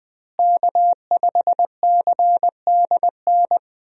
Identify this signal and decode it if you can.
Morse code, heard as a keyed tone, about 20 wpm, 700 Hz. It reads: K5CDN
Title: USA Call Signs - 5 Characters - Repeated 3x - 20wpm